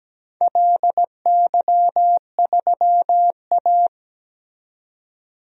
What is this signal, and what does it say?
Morse code, heard as a keyed tone, about 17 wpm, 700 Hz. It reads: LY3A
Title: DXCC-100 Call Signs - 4 Characters - 17wpm